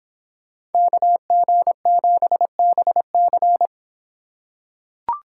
Morse code, heard as a keyed tone, about 25 wpm, 700 Hz. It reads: KG7BC